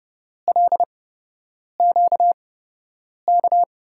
Morse code, heard as a keyed tone, about 30 wpm, spaced as for 8 wpm, 700 Hz. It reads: LQK
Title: USA Call Signs - 5 Characters - 8x Character Spacing - 30wpm